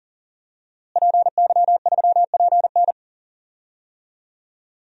Morse code, heard as a keyed tone, about 39 wpm, 700 Hz. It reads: PY3PN
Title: DXCC-100 Call Signs - 5 Characters - Repeated 2x - 40wpm